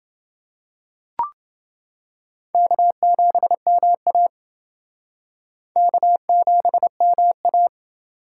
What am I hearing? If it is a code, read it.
Morse code, heard as a keyed tone, about 28 wpm, 700 Hz. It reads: K7MA K7MA